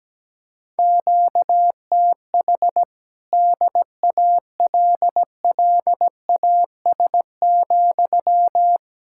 Morse code, heard as a keyed tone, about 17 wpm, 700 Hz. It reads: QTH DALLAS,
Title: Speed-Racing Common QSO Phrases - 17wpm